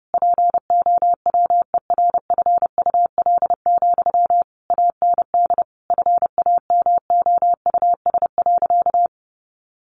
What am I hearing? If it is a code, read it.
Morse code, heard as a keyed tone, about 30 wpm, 700 Hz. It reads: POWERFUL, AND FAMOUS.
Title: Sentences from Top 1500 Words 30wpm